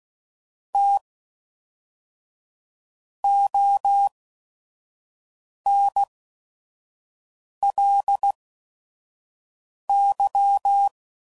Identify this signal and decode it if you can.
Morse code, heard as a keyed tone, about 16 wpm, spaced as for 5 wpm, 800 Hz. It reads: T ONLY
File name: CW Code Groups @5WPM / Technical Subjects / AND Gate